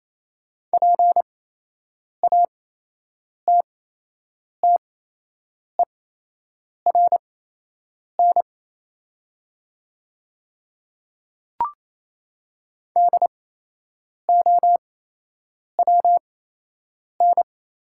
Morse code, heard as a keyed tone, about 28 wpm, spaced as for 8 wpm, 700 Hz. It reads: PATTERN DOWN
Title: 8x Character Spacing - Top 500 Words - 28wpm